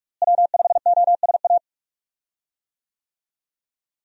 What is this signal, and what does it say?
Morse code, heard as a keyed tone, about 41 wpm, 700 Hz. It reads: W5OSA
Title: USA Call Signs - 5 Characters - Repeated 2x - 45wpm